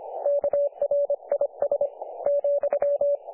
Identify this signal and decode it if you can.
Morse code, heard as a keyed tone, about 25 wpm, 500 Hz. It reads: KRIS ,
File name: dlazaawansowanych / nagraniazpasma / websdr recording 3564.7kHz